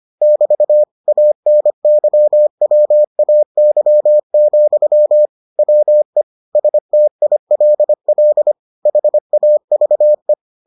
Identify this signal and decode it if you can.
Morse code, heard as a keyed tone, about 25 wpm, 600 Hz. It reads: <BT> ANYWAY, WE STILL HAVE